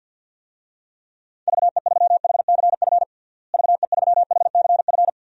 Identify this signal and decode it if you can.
Morse code, heard as a keyed tone, about 53 wpm, 700 Hz. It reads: VE3HCF VE3HCF
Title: DXCC-100 Call Signs - 6 Characters - Repeated 3x - 50wpm